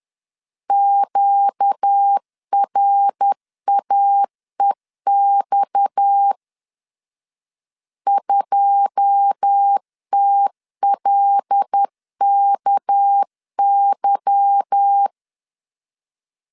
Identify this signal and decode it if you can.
Morse code, heard as a keyed tone, about 10 wpm, 800 Hz. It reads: QRAEX 2TLKY
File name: LEZIONI / 10 SIGNS / 10C DOT 10wpm 2940